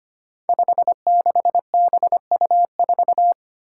Morse code, heard as a keyed tone, about 25 wpm, 700 Hz. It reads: 56BU4
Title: Five Challenging Letter-Number Combinations 25wpm